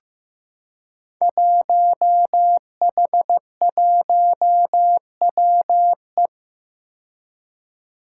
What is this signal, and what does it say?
Morse code, heard as a keyed tone, about 15 wpm, 700 Hz. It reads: 1H1WE
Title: Five Challenging Letter-Number Combinations 15wpm